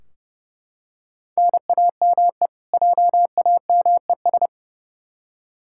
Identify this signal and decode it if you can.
Morse code, heard as a keyed tone, about 32 wpm, 700 Hz. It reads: NAME JAMES
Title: Practice QSO - Exchange 1 - Beginning - 30wpm